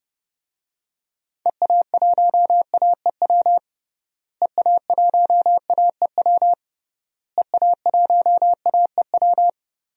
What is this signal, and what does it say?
Morse code, heard as a keyed tone, about 30 wpm, 700 Hz. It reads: EA1AEW EA1AEW EA1AEW
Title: DXCC-100 Call Signs - 6 Characters - Repeated 3x - 30wpm